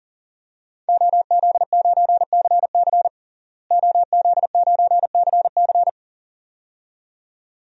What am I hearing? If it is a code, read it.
Morse code, heard as a keyed tone, about 37 wpm, 700 Hz. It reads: OZ9CC OZ9CC
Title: DXCC-100 Call Signs - 5 Characters - Repeated 2x - 40wpm